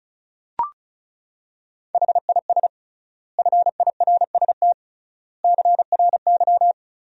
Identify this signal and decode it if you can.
Morse code, heard as a keyed tone, about 36 wpm, 700 Hz. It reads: HIS FIRST CRY
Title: 3x Word-Spacing - Sentences from Top 500 Words - 35wpm